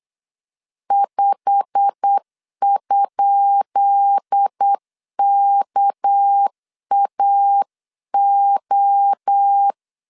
Morse code, heard as a keyed tone, about 8 wpm, 800 Hz. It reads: 5?KAO
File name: LEZIONI / 10 SIGNS / 10B INT 8wpm 3534